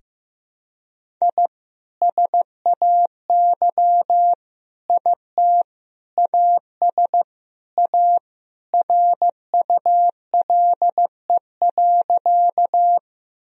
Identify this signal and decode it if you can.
Morse code, heard as a keyed tone, about 16 wpm, 700 Hz. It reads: I SAY IT AS A RULE.
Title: Sentences from Top 1000 Words - Review Part 2 - 15wpm